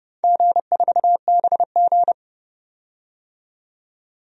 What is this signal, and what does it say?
Morse code, heard as a keyed tone, about 30 wpm, 700 Hz. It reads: G4BG
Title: DXCC-100 Call Signs - 4 Characters - 30wpm